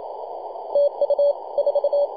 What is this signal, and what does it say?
Morse code, heard as a keyed tone, about 27 wpm, 570 Hz. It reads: TU 4
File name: 2229z 4W/JI1AVY 21.023 CW